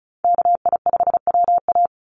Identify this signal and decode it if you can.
Morse code, heard as a keyed tone, about 34 wpm, 700 Hz. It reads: KI5WA
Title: NA SOTA Chaser Callsigns 35wpm